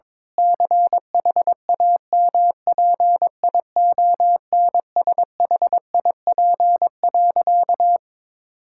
Morse code, heard as a keyed tone, about 22 wpm, 700 Hz. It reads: CHAMPIONSHIP.
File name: practice / 22wpm / Sentences with 8-Word Phrases - 22wpm